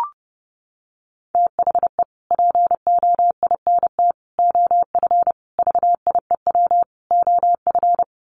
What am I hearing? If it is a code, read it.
Morse code, heard as a keyed tone, about 30 wpm, 700 Hz. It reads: THE POINT OF VIEW OF
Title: Sets of 5 Words 30wpm